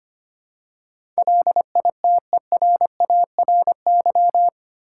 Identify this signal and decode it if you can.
Morse code, heard as a keyed tone, about 25 wpm, 700 Hz. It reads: LITERARY
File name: 8 Letter Words - First 3 Letters Given - 25wpm